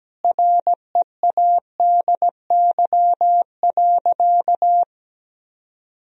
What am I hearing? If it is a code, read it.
Morse code, heard as a keyed tone, about 17 wpm, 700 Hz. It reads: READY.
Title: Sentences from Top 1000 Words - Review Part 1 - 17wpm